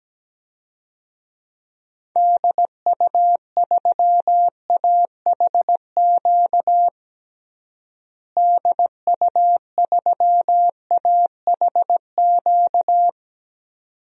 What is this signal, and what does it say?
Morse code, heard as a keyed tone, about 17 wpm, 700 Hz. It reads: DU3AHQ DU3AHQ
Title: DXCC-100 Call Signs - 6 Characters - Repeated 2x - 17wpm